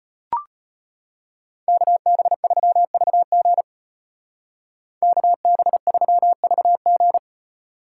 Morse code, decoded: KB3VG KB3VG

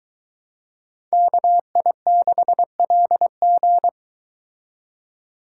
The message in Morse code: KI6LG

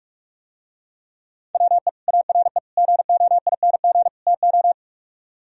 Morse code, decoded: WE ARE GOING TO

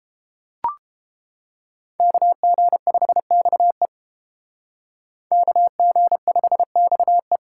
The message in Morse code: KG5XE KG5XE

